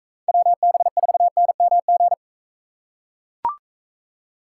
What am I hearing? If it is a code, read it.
Morse code, heard as a keyed tone, about 42 wpm, 700 Hz. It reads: WB4NMG